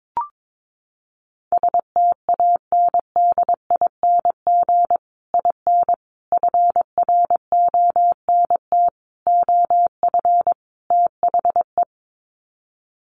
Morse code, heard as a keyed tone, about 22 wpm, 700 Hz. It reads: STANDING IN FRONT OF THE
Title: Sets of 5 Words Encore2 22wpm